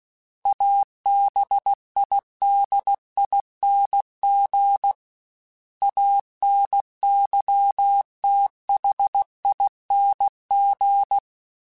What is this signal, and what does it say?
Morse code, heard as a keyed tone, about 16 wpm, 800 Hz. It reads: ABIDING ANYTHING
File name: Special INGs / Special INGs-mp3s / Sngl ing @15